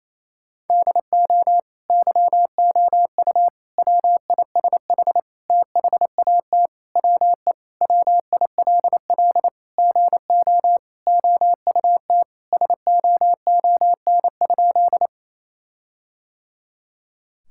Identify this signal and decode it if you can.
Morse code, heard as a keyed tone, about 28 wpm, 700 Hz. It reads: DO YOU WISH THAT WE WILL GO OUT SOON?
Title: Sentences from Top 1600 Words 28wpm